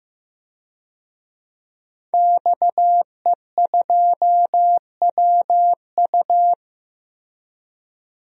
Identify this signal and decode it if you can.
Morse code, heard as a keyed tone, about 15 wpm, 700 Hz. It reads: XE2WU
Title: DXCC-100 Call Signs - 5 Characters - 15wpm